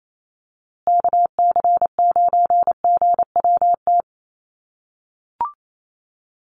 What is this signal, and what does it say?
Morse code, heard as a keyed tone, about 28 wpm, 700 Hz. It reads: KC9GWT
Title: USA Call Signs Repeated 3x 28wpm